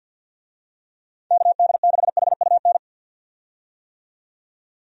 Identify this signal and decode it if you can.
Morse code, heard as a keyed tone, about 51 wpm, 700 Hz. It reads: KD6HUN
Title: USA Call Signs - 6 Characters - 50wpm